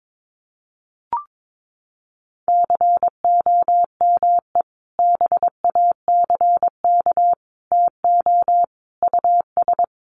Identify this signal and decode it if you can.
Morse code, heard as a keyed tone, about 22 wpm, 700 Hz. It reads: COME BACK TO US
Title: Sentences from Top 500 Words 22wpm